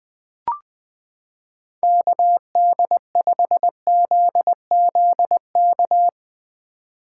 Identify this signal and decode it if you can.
Morse code, heard as a keyed tone, about 20 wpm, 700 Hz. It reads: KD5ZZK
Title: 2023 SOTA Activator Call Signs - 20wpm